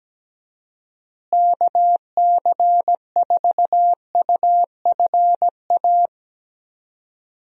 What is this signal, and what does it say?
Morse code, heard as a keyed tone, about 17 wpm, 700 Hz. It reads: KC4UFA
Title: USA Call Signs - 6 Characters - Repeated 2x - 17wpm